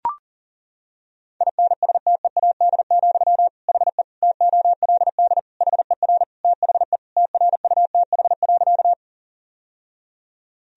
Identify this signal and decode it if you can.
Morse code, heard as a keyed tone, about 37 wpm, 700 Hz. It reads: INSTEAD, HE TOLD HER THE TRUTH.
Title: Sentences from Top 1000 Words - Review Part 4 - 40wpm